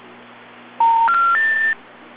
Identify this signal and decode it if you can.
Special information tone.